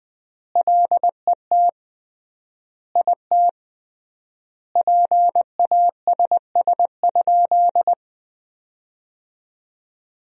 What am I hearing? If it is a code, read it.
Morse code, heard as a keyed tone, about 21 wpm, 700 Hz. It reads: LET IT PASS?
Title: 3x Word-Spacing - Sentences from Top 400 Words - 20wpm